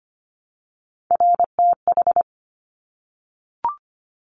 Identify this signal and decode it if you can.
Morse code, heard as a keyed tone, about 24 wpm, 700 Hz. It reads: RTH
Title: Top Three Letter Combinations in Words 25wpm